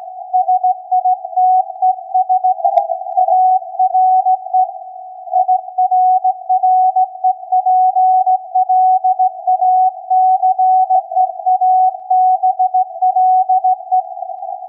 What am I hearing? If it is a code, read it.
Morse code, heard as a keyed tone, about 16 wpm, 740 Hz. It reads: SITES ARE IRREPLACEABLE
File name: iz7slz / VLF / SAQ 16-11-2022